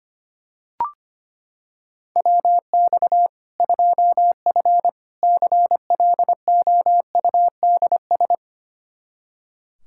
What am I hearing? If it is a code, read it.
Morse code, heard as a keyed tone, about 25 wpm, 700 Hz. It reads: WX 2F CLOUDS
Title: Practice QSO - Exchange 2 - Info - 25wpm